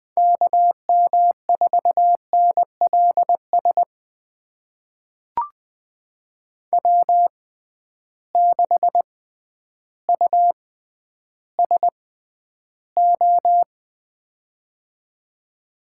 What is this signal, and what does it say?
Morse code, heard as a keyed tone, about 20 wpm, spaced as for 11 wpm, 700 Hz. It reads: KM4NLS W6USO
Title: USA Call Signs - 6x Character Spacing - 20wpm